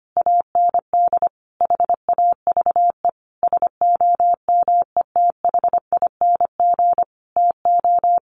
Morse code, heard as a keyed tone, about 25 wpm, 700 Hz. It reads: AND HAVE SOMETHING TO
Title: Sentences from Top 600 Words 25wpm